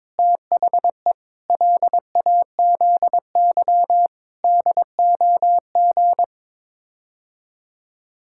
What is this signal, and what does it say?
Morse code, heard as a keyed tone, about 22 wpm, 700 Hz. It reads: THE LAZY DOG
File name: practice / 22wpm / Sets of 8 Words - 22wpm